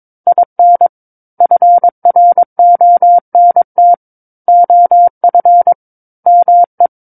Morse code, heard as a keyed tone, about 22 wpm, 700 Hz. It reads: IN FRONT OF ME
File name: Book One / 13-Chapter 13 - 22 WPM